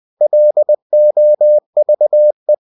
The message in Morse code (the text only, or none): LOVE